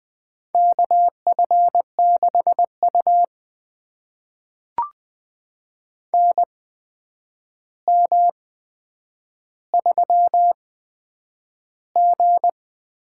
Morse code, heard as a keyed tone, about 20 wpm, spaced as for 9 wpm, 700 Hz. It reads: KF6U NM3G
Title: USA Call Signs - 4 Characters - 8x Character Spacing - 20wpm